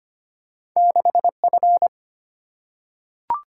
Morse code, heard as a keyed tone, about 25 wpm, 700 Hz. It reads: 6 F
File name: Two Letter-Number Combinations - 25wpm